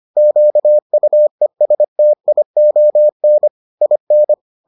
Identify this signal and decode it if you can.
Morse code, heard as a keyed tone, about 25 wpm, 600 Hz. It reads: QUESTION IN